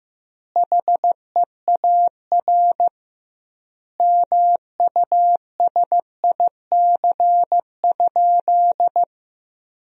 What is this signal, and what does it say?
Morse code, heard as a keyed tone, about 15 wpm, 700 Hz. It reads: HEAR MUSIC?
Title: 2x Word-Spacing - Sentences from Top 300 Words - 15wpm